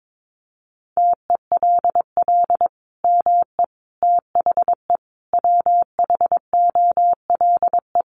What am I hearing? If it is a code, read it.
Morse code, heard as a keyed tone, about 22 wpm, 700 Hz. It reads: TELL ME THE WHOLE
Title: Sentences from Top 800 Words 22wpm